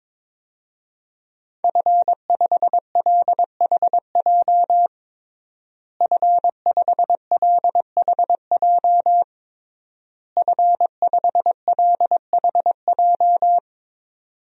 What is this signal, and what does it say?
Morse code, heard as a keyed tone, about 21 wpm, 700 Hz. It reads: F5LHJ F5LHJ F5LHJ
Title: DXCC-100 Call Signs - 5 Characters - Repeated 3x - 22wpm